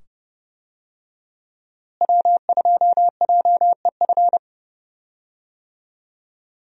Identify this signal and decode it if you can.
Morse code, heard as a keyed tone, about 30 wpm, 700 Hz. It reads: W2JEF
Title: USA Call Signs - 5 Characters - Repeated 2x - 30wpm